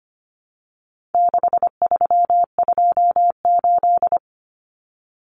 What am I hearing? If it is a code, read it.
Morse code, heard as a keyed tone, about 25 wpm, 700 Hz. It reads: 6328
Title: Common Cut Numbers - 4-Digit 25wpm